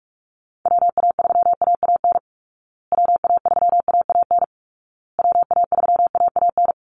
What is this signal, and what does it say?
Morse code, heard as a keyed tone, about 44 wpm, 700 Hz. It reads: WA3AAN WA3AAN WA3AAN
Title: Call Signs Repeated 2x - 45wpm